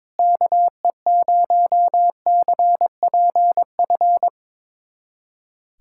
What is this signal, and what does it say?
Morse code, heard as a keyed tone, about 22 wpm, 700 Hz. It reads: KE0CPF
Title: USA Callsign DE Callsign - 6 Characters - 22wpm